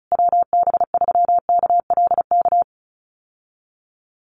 Morse code, heard as a keyed tone, about 34 wpm, 700 Hz. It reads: WB3KLK
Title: USA Call Signs Repeated 3x 35wpm